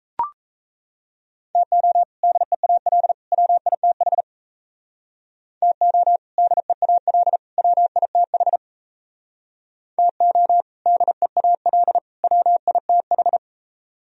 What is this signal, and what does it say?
Morse code, heard as a keyed tone, about 38 wpm, 700 Hz. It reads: TO DEAL WITH TO DEAL WITH TO DEAL WITH